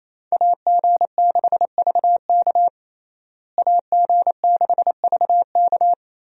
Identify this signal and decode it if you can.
Morse code, heard as a keyed tone, about 27 wpm, 700 Hz. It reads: AG6VK AG6VK
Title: USA Call Signs - 5 Characters - Repeated 3x - 28wpm